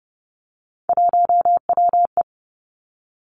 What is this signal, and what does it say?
Morse code, heard as a keyed tone, about 30 wpm, 700 Hz. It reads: 1WE
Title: Three Challenging Letter-Number Combinations 30wpm